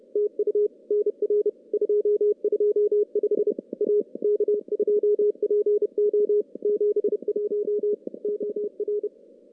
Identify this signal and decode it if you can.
Morse code, heard as a keyed tone, about 31 wpm, 420 Hz. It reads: TU NR 225 A K2PO 71 OR
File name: contests / CW-SS-k5tr- -0000